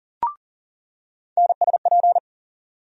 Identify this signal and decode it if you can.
Morse code, heard as a keyed tone, about 40 wpm, 700 Hz. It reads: NSP
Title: USA Call Sign Postfixes - 3 Letters 40wpm